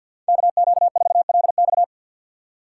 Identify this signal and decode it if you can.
Morse code, heard as a keyed tone, about 52 wpm, 700 Hz. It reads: KQ4LX
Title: USA Call Signs - 5 Characters - 4x Character Spacing - 50wpm